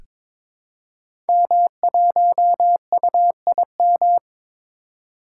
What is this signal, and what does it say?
Morse code, heard as a keyed tone, about 22 wpm, 700 Hz. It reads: M1UIM
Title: Code Groups - 5 Letter:Numbers - 22wpm